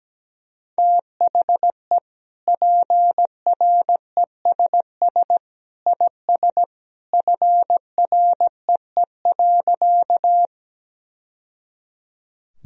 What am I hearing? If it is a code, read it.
Morse code, heard as a keyed tone, about 17 wpm, 700 Hz. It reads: THE PRESS IS FREE.
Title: Sentences from Top 1000 Words - Review Part 1 - 17wpm